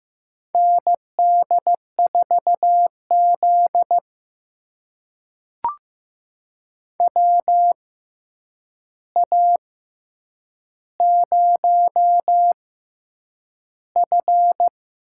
Morse code, heard as a keyed tone, about 15 wpm, spaced as for 8 wpm, 700 Hz. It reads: ND4Z WA0F
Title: USA Call Signs - 4 Characters - 6x Character Spacing - 15wpm